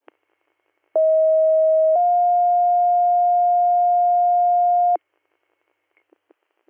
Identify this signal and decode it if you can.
Two-tone page, heard as a one-second tone, then a three-second tone, about 640 then 710 Hz. It is heard from 1.0 to 5.0 s.